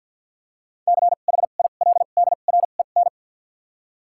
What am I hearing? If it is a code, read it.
Morse code, heard as a keyed tone, about 53 wpm, spaced as for 37 wpm, 700 Hz. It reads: CHILDREN
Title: 2x Character Spacing - Top 500 Words - 50wpm